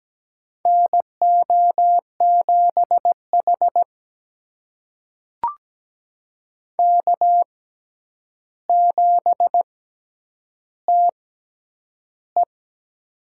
Morse code, heard as a keyed tone, about 17 wpm, spaced as for 9 wpm, 700 Hz. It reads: NO7H K7TE
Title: USA Call Signs - 6x Character Spacing - 17wpm